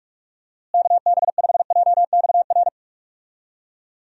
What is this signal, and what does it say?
Morse code, heard as a keyed tone, about 42 wpm, 700 Hz. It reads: KB5JXR
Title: USA Call Signs - 6 Characters - Repeated 3x - 45wpm